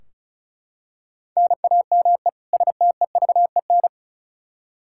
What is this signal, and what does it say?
Morse code, heard as a keyed tone, about 36 wpm, 700 Hz. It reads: NAME STEVEN
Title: Practice QSO - Exchange 1 - Beginning - 35wpm